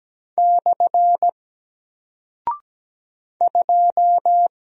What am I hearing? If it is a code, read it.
Morse code, heard as a keyed tone, about 17 wpm, 700 Hz. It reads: / 2
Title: Single Character (Letter:Number:Punctuation:Prosign) 17wpm